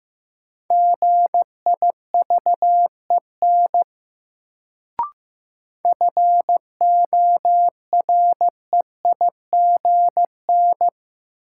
Morse code, heard as a keyed tone, about 15 wpm, 700 Hz. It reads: GIVEN FOREIGN
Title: Sentences from Top 1400 Words - New Word Drill Alt - 15wpm